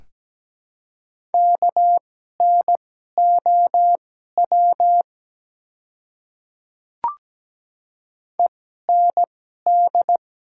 Morse code, heard as a keyed tone, about 17 wpm, spaced as for 12 wpm, 700 Hz. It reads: KNOW END
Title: 2x Character Spacing - Top 200 Words - 17wpm